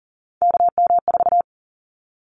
Morse code, heard as a keyed tone, about 40 wpm, 700 Hz. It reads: KM4
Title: Callsign Prefixes for Top 25 DXCC - 40wpm